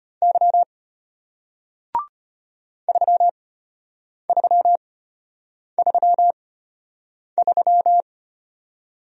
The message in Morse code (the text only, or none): Y 3333